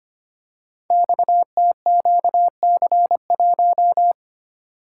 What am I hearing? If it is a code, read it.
Morse code, heard as a keyed tone, about 25 wpm, 700 Hz. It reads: XTQC1
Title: Code Groups - 5 Letter:Numbers - 25wpm